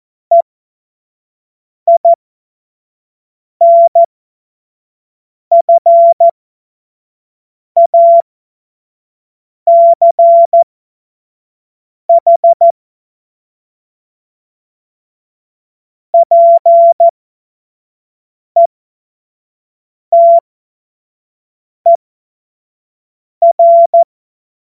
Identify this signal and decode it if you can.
Morse code, heard as a keyed tone, about 14 wpm, spaced as for 5 wpm, 680 Hz. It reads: EINFACH PETER
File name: Morsekurs / Witze / Witz 30. Dezember 2025 25 F70